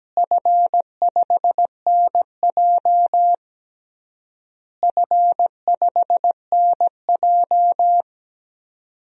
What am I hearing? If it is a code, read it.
Morse code, heard as a keyed tone, about 17 wpm, 700 Hz. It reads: F5NJ F5NJ
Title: DXCC-100 Call Signs - 4 Characters - Repeated 3x - 17wpm